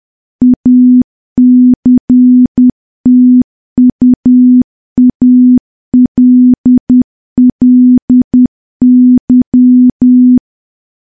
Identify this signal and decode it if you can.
Morse code, heard as a keyed tone, about 10 wpm, 250 Hz. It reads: ACTUALLY
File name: morse / morse-training-10WPM